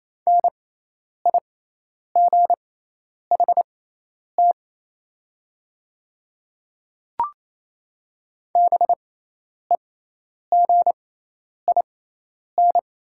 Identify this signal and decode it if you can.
Morse code, heard as a keyed tone, about 28 wpm, spaced as for 10 wpm, 700 Hz. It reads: NIGHT BEGIN
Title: 6x Character Spacing - Top 500 Words - 28wpm